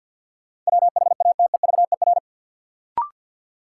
Morse code, heard as a keyed tone, about 52 wpm, 700 Hz. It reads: WHATEVER
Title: Sentences from Top 1100 Words - New Word Drill Alt - 50wpm